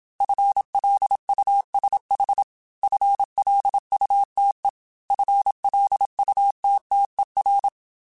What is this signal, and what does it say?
Morse code, heard as a keyed tone, about 27 wpm, 800 Hz. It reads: FLUSH FLUTE FLUTTER